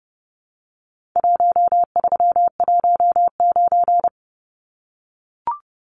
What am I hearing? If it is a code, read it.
Morse code, heard as a keyed tone, about 30 wpm, 700 Hz. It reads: 1319
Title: Numbers - Contest Style - 4-Digit 30wpm